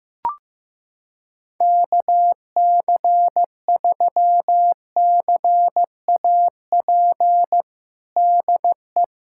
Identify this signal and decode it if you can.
Morse code, heard as a keyed tone, about 15 wpm, 700 Hz. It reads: KC3CAP DE
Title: USA Callsign DE Callsign 15wpm